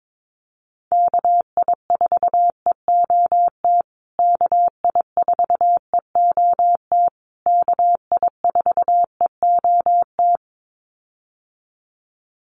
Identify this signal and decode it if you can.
Morse code, heard as a keyed tone, about 22 wpm, 700 Hz. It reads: KI4EOT KI4EOT KI4EOT
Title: USA Call Signs Repeated 3x 22wpm